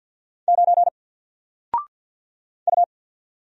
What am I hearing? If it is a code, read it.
Morse code, heard as a keyed tone, about 50 wpm, 700 Hz. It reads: 9 U
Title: Single Letter-Number Alt 50wpm